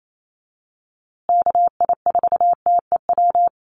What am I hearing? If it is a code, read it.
Morse code, heard as a keyed tone, about 28 wpm, 700 Hz. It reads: KI4TEW
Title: USA Call Signs Repeated 3x 28wpm